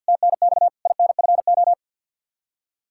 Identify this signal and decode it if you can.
Morse code, heard as a keyed tone, about 52 wpm, 700 Hz. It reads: TNX INFO
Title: Practice QSO - Exchange 3 - More Info - 50wpm